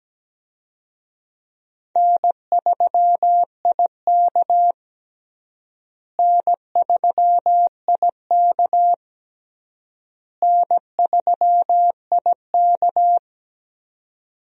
Morse code, heard as a keyed tone, about 17 wpm, 700 Hz. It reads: N3IK N3IK N3IK
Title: USA Call Signs - 4 Characters - Repeated 3x - 17wpm